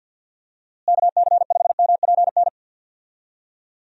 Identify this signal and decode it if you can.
Morse code, heard as a keyed tone, about 51 wpm, 700 Hz. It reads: KC5MPN
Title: USA Call Signs - 6 Characters - Repeated 2x - 50wpm